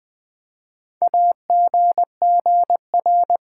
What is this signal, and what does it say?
Morse code, heard as a keyed tone, about 20 wpm, 700 Hz. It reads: AGGR
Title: Build A Word - 4 Letters At A Time - 10-Letter Words - 20wpm